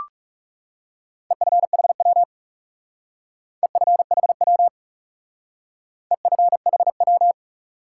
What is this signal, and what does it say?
Morse code, heard as a keyed tone, about 36 wpm, 700 Hz. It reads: EFHW EFHW EFHW
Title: Speed-Racing Top 180 QSO Elements 30wpm